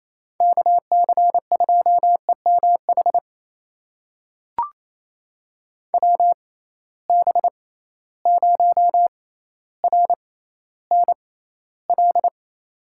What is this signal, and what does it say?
Morse code, heard as a keyed tone, about 28 wpm, spaced as for 14 wpm, 700 Hz. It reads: KC2EMH WB0RNL
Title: USA Call Signs - 6 Characters - 6x Character Spacing - 28wpm